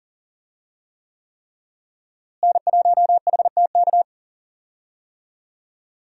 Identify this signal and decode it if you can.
Morse code, heard as a keyed tone, about 40 wpm, 700 Hz. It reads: N1HTK
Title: USA Call Signs - 5 Characters - Repeated 3x - 40wpm